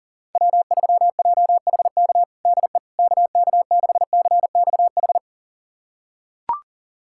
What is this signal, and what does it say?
Morse code, heard as a keyed tone, about 40 wpm, 700 Hz. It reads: W3JHK DE KK6CXH
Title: USA Callsign DE Callsign 40wpm